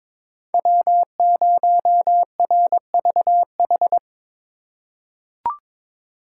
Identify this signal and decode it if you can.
Morse code, heard as a keyed tone, about 22 wpm, 700 Hz. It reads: W0RVH
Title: USA Call Signs - 5 Characters - 3x Character Spacing - 22wpm